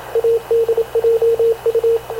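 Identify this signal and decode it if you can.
Morse code, heard as a keyed tone, about 26 wpm, 460 Hz, with radio noise. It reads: ADJU
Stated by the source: Havana Radio / CLA sending Spanish press (PX) in Morse, recorded by Radio Officer Paul Dumesnil onboard CNAV QUEST/CZDO.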